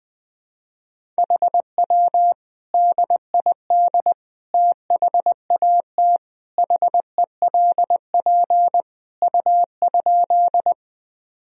Morse code, heard as a keyed tone, about 20 wpm, 700 Hz. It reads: HW DID THAT HELP U?
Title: Sentences from Top 200 Words and Abbreviations 20wpm